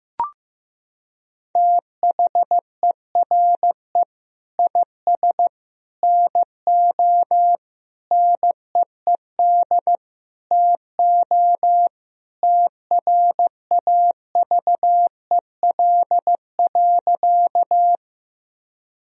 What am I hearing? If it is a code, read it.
Morse code, heard as a keyed tone, about 15 wpm, 700 Hz. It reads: THERE IS NO NEED TO TRAVEL.
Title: Sentences from Top 1000 Words - Review Part 2 - 15wpm